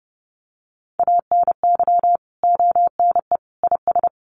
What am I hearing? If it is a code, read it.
Morse code, heard as a keyed tone, about 30 wpm, 700 Hz. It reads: ANY ONE IS
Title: Sentences from Top 1200 Words 30wpm